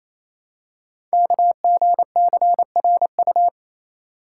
Morse code, heard as a keyed tone, about 27 wpm, 700 Hz. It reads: KGCRU
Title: Code Groups - 5 Letters - 28wpm